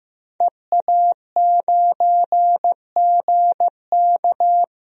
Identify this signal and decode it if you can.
Morse code, heard as a keyed tone, about 15 wpm, 700 Hz. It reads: EA9GK
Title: DXCC-100 Call Signs - 5 Characters - Repeated 2x - 15wpm